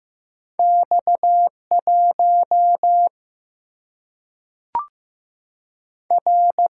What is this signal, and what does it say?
Morse code, heard as a keyed tone, about 15 wpm, 700 Hz. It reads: X 1 R
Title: Single-Letter Single-Number 15wpm